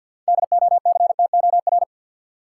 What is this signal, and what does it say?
Morse code, heard as a keyed tone, about 53 wpm, 700 Hz. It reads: DOCTOR
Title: Practice QSO - Exchange 3 - More Info - 50wpm